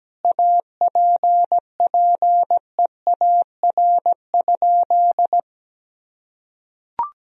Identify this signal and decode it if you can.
Morse code, heard as a keyed tone, about 17 wpm, 700 Hz. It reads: APPEAR?
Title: Sentences from Top 1000 Words - Review Part 2 - 17wpm